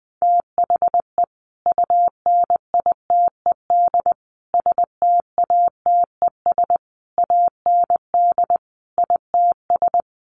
Morse code, heard as a keyed tone, about 19 wpm, 700 Hz. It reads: THE UNITED STATES AND ITS
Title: Sets of 5 Words Encore 20wpm